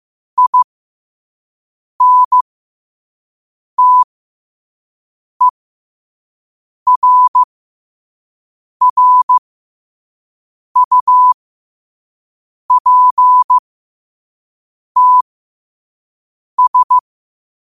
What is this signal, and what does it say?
Morse code, heard as a keyed tone, about 15 wpm, spaced as for 5 wpm, 1000 Hz. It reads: INTERRUPTS
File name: qotdfiles / 5wpm